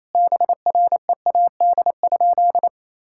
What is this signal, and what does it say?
Morse code, heard as a keyed tone, about 28 wpm, 700 Hz. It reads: BREAD?
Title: Sentences from Top 1600 Words 28wpm